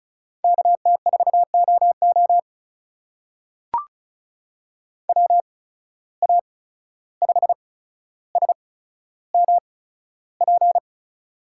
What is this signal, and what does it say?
Morse code, heard as a keyed tone, about 36 wpm, spaced as for 14 wpm, 700 Hz. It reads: KT4OO WA5SMP
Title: USA Call Signs - 8x Character Spacing - 35wpm